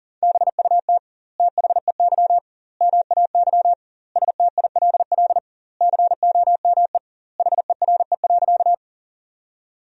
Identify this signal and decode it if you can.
Morse code, heard as a keyed tone, about 40 wpm, 700 Hz. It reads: BUT THEY MAY STILL COME HERE.
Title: 2x Word-Spacing - Sentences from Top 300 Words - 40wpm